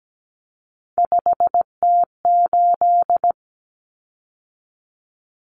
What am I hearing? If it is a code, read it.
Morse code, heard as a keyed tone, about 17 wpm, 700 Hz. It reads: 5T8
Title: Common Cut Numbers - 3-Digit 17wpm